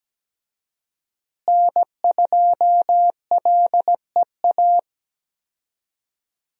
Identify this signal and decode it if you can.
Morse code, heard as a keyed tone, about 17 wpm, 700 Hz. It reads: N2LEA
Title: USA Call Signs - 5 Characters - Repeated 2x - 17wpm